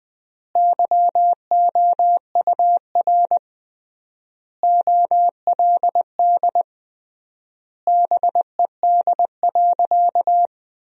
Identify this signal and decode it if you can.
Morse code, heard as a keyed tone, about 20 wpm, 700 Hz. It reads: YOUR OLD BED.
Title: 3x Word-Spacing - Sentences from Top 500 Words - 20wpm